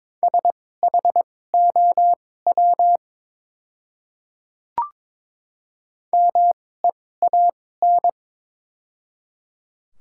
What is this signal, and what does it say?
Morse code, heard as a keyed tone, about 22 wpm, spaced as for 16 wpm, 700 Hz. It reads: SHOW MEAN
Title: 2x Character Spacing - Top 200 Words - 22wpm